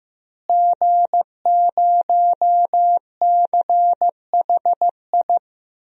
Morse code, heard as a keyed tone, about 15 wpm, 700 Hz. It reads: G0CHI
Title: DXCC-100 Call Signs - 5 Characters - Repeated 2x - 15wpm